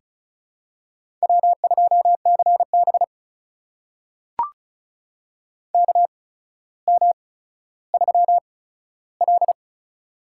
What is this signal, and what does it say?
Morse code, heard as a keyed tone, about 36 wpm, spaced as for 15 wpm, 700 Hz. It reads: W2CB KM3L
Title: USA Call Signs - 8x Character Spacing - 35wpm